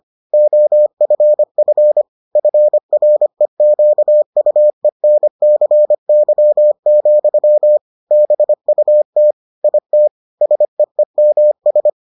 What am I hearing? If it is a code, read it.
Morse code, heard as a keyed tone, about 25 wpm, 600 Hz. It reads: OFF FREQUENCY, BUT IT SEEMS